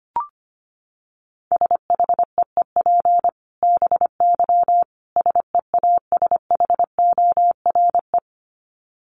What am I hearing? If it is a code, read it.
Morse code, heard as a keyed tone, about 25 wpm, 700 Hz. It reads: SHEEP BY SEASHORE
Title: Dit Challenge 25wpm